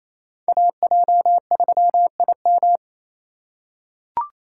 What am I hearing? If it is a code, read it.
Morse code, heard as a keyed tone, about 28 wpm, 700 Hz. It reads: AJ3IM
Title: USA Call Signs - 5 Characters - 4x Character Spacing - 28wpm